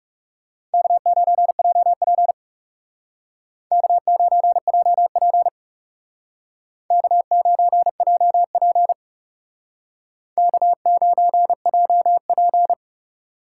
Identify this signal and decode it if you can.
Morse code, heard as a keyed tone, about 38 wpm, 700 Hz. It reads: K9JP K9JP K9JP K9JP